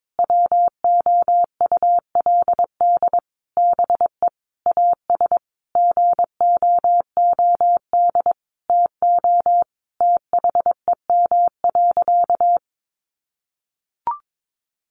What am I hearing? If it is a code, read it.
Morse code, heard as a keyed tone, about 22 wpm, 700 Hz. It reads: WOULD BE AS GOOD TO THEM.
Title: Sentences from Top 1200 Words 22wpm